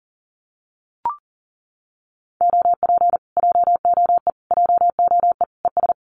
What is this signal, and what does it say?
Morse code, heard as a keyed tone, about 40 wpm, 700 Hz. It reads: OP JOE JOE ES
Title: Common QSO Phrases Encore 40wpm